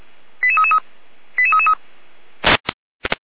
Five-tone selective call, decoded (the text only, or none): 88333 88333